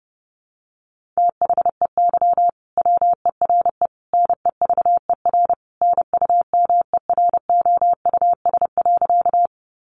Morse code, heard as a keyed tone, about 30 wpm, 700 Hz. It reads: THEY WERE NEVER NUMEROUS.
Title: Sentences from Top 1400 Words 30wpm